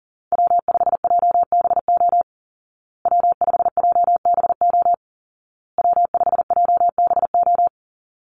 Morse code, heard as a keyed tone, about 40 wpm, 700 Hz. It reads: W5JBO W5JBO W5JBO
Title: Call Signs Repeated 2x - 40wpm